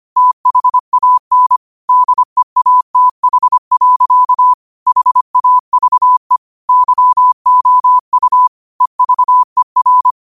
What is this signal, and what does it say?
Morse code, heard as a keyed tone, about 25 wpm, 1000 Hz. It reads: THAN DEATH. HAVE YOU EVER
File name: qotdfiles / 25wpm